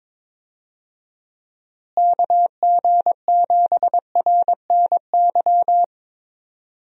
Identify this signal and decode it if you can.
Morse code, heard as a keyed tone, about 22 wpm, 700 Hz. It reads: KG7RNY
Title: USA Call Signs - 6 Characters - Repeated 3x - 22wpm